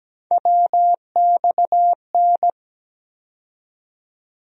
Morse code, heard as a keyed tone, about 17 wpm, 700 Hz. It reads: WXN
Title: USA Call Sign Postfixes - 3 Letters 17wpm